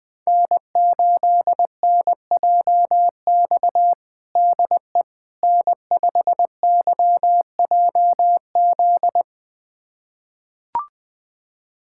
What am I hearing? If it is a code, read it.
Morse code, heard as a keyed tone, about 20 wpm, 700 Hz. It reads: N8NJX DE N5YJZ
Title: USA Callsign DE Callsign - 5 Characters - 20wpm